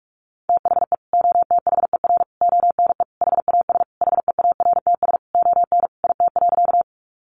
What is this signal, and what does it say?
Morse code, heard as a keyed tone, about 44 wpm, 700 Hz. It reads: THE OTHER ONE HAS HEARTS ON IT.
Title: Additional Easy Phrases 45wpm